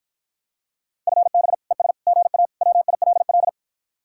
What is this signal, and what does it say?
Morse code, heard as a keyed tone, about 53 wpm, 700 Hz. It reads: FB ES GA WILL